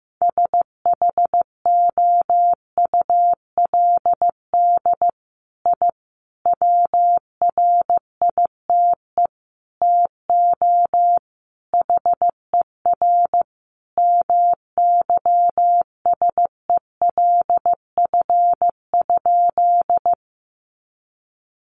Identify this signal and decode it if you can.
Morse code, heard as a keyed tone, about 15 wpm, 700 Hz. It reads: SHOULD I WRITE TO HER MYSELF?
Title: Sentences from Top 1300 Words 15wpm